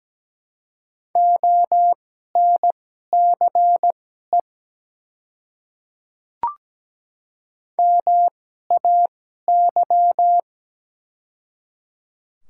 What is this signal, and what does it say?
Morse code, heard as a keyed tone, about 17 wpm, spaced as for 12 wpm, 700 Hz. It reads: ONCE MAY
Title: 2x Character Spacing - Top 300 Words - 17wpm